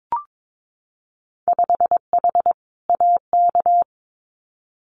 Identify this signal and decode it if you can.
Morse code, heard as a keyed tone, about 22 wpm, 700 Hz. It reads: 5H AK
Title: Winter Field Day Exchanges Encore 22wpm